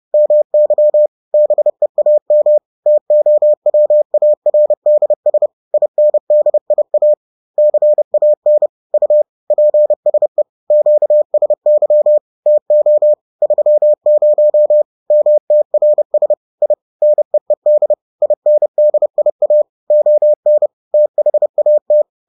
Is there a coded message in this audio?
Yes: Morse code from 0.1 to 22.0 s, heard as a keyed tone, about 30 wpm, 600 Hz. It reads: MY BEAM TOWARDS INDIA CAN U PSE QSY TO 30 MTRS I NEED INDIA ON THAT